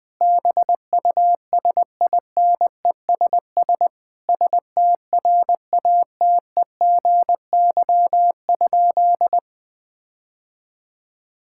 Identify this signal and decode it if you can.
Morse code, heard as a keyed tone, about 20 wpm, 700 Hz. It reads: BUSINESS STRATEGY?
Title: Sentences from Top 1000 Words - Review Part 6 - 20wpm